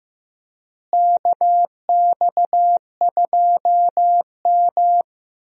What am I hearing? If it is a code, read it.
Morse code, heard as a keyed tone, about 15 wpm, 700 Hz. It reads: KX2M
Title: USA Call Signs - 4 Characters - Repeated 3x - 15wpm